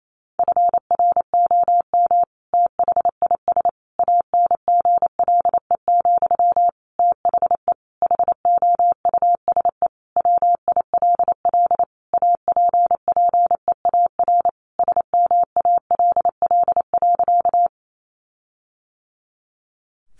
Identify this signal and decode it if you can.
Morse code, heard as a keyed tone, about 28 wpm, 700 Hz. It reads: FROM THIS ANGLE, THE HOUSE WILL APPEAR SMALL.
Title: Sentences from Top 1500 Words 28wpm